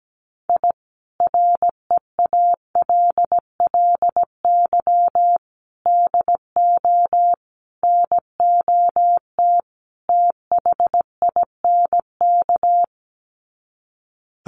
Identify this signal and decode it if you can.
Morse code, heard as a keyed tone, about 17 wpm, 700 Hz. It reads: I REALLY DO NOT THINK
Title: Sets of 5 Words Encore2 17wpm